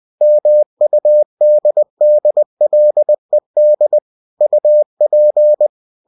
Morse code, heard as a keyed tone, about 20 wpm, 600 Hz. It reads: MUDDLED UP
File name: cw-books / winnie-the-pooh / winnie-20wpm-page-0024